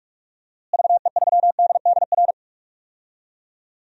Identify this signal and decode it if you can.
Morse code, heard as a keyed tone, about 43 wpm, 700 Hz. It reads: VE3DDR
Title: DXCC-100 Call Signs - 6 Characters - Repeated 3x - 45wpm